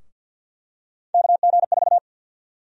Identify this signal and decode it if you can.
Morse code, heard as a keyed tone, about 52 wpm, 700 Hz. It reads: KG4
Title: USA Call Sign Prefixes - Repeated 2x - 50wpm